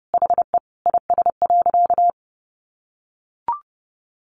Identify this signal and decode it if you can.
Morse code, heard as a keyed tone, about 30 wpm, 700 Hz. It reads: HE IS.
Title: Sentences from Top 1100 Words 30wpm